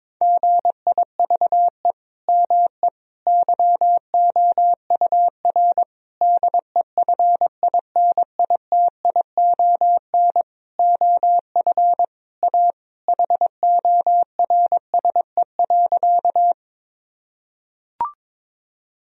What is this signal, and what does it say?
Morse code, heard as a keyed tone, about 22 wpm, 700 Hz. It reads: GIVE ME YOUR DEFINITION OF A HORSE.
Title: Sentences from Top 1000 Words - Review Part 4 - 22wpm